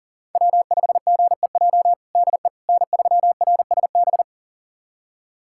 Morse code, heard as a keyed tone, about 40 wpm, 700 Hz. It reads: W5GEJ DE N3RSB
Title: USA Callsign DE Callsign - 5 Characters - 40wpm